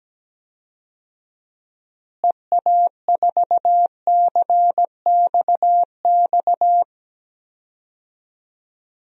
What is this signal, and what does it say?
Morse code, heard as a keyed tone, about 17 wpm, 700 Hz. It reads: EA4CXX
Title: DXCC-100 Call Signs - 6 Characters - Repeated 3x - 17wpm